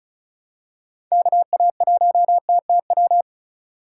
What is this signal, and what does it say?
Morse code, heard as a keyed tone, about 36 wpm, 700 Hz. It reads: KA1TTW
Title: USA Call Signs - 6 Characters - 8x Character Spacing - 35wpm